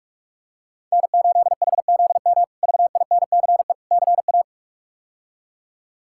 Morse code, heard as a keyed tone, about 47 wpm, 700 Hz. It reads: N8HZM VINCE CA
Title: K1USN SST QSO Practice - 45wpm